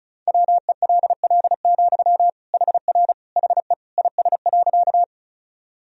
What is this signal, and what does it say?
Morse code, heard as a keyed tone, about 34 wpm, 690 Hz. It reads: WELL, HR HE IS.
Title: Sentences from Top 200 Words and Abbreviations 35wpm